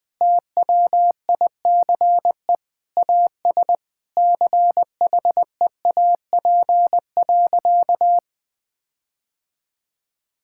Morse code, heard as a keyed tone, about 20 wpm, 700 Hz. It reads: TWICE AS CHEAP.
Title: Sentences from Top 1600 Words 20wpm